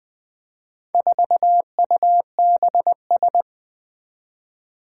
Morse code, heard as a keyed tone, about 20 wpm, 700 Hz. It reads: 4UBS
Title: Four Challenging Letter-Number Combinations 20wpm